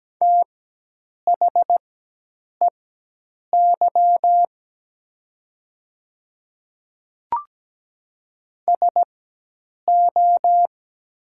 Morse code, heard as a keyed tone, about 17 wpm, spaced as for 8 wpm, 700 Hz. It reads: THEY SO
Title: 4x Character Spacing - Top 200 Words - 17wpm